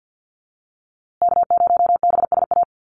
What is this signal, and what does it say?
Morse code, heard as a keyed tone, about 48 wpm, 700 Hz. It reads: K0BSA
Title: USA Call Signs Repeated 3x 50wpm